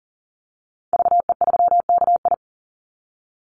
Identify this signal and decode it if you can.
Morse code, heard as a keyed tone, about 40 wpm, 700 Hz. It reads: VE3KI
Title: Call Signs - 40wpm